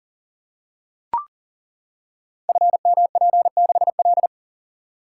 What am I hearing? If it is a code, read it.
Morse code, heard as a keyed tone, about 42 wpm, 700 Hz. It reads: FMP6L
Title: Code Groups - 5 Letter:Numbers - 40wpm